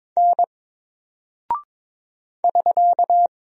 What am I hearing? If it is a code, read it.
Morse code, heard as a keyed tone, about 22 wpm, 700 Hz. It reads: N <SK>
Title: Single Character (Letter:Number:Punctuation:Prosign) 22wpm